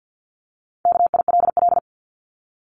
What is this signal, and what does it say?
Morse code, heard as a keyed tone, about 50 wpm, 700 Hz. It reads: KILL
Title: Top 1251-1500 Words 50wpm